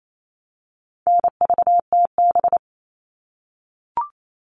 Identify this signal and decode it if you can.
Morse code, heard as a keyed tone, about 28 wpm, 700 Hz. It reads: NVTB
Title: All Cut Numbers - 4-Digit 28wpm